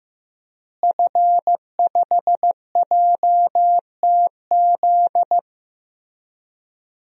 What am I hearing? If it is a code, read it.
Morse code, heard as a keyed tone, about 15 wpm, 700 Hz. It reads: F5JTZ
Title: DXCC-100 Call Signs - 5 Characters - Repeated 2x - 15wpm